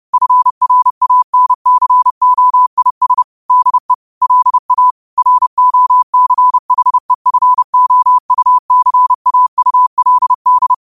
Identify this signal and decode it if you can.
Morse code, heard as a keyed tone, about 30 wpm, 1000 Hz. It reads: FRANCOIS DE LA ROCHEFOUCAULD
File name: qotdfiles / 30wpm